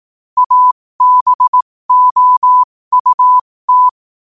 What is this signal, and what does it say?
Morse code, heard as a keyed tone, about 18 wpm, spaced as for 15 wpm, 1000 Hz. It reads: ABOUT
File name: qotdfiles / 15wpm